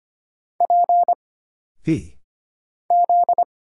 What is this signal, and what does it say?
Morse code, heard as a keyed tone, about 25 wpm, 700 Hz. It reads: P Z
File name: Single Character (Letter:Number:Punctuation:Prosign) - Mind-Melt - 25wpm